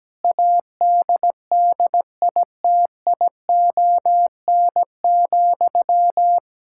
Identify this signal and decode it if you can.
Morse code, heard as a keyed tone, about 17 wpm, 700 Hz. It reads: ADDITION,
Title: Sentences from Top 1000 Words - Review Part 6 - 17wpm